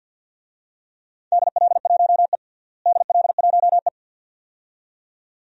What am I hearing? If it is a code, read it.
Morse code, heard as a keyed tone, about 52 wpm, 700 Hz. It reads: DL1E DL1E
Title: DXCC-100 Call Signs - 4 Characters - Repeated 2x - 50wpm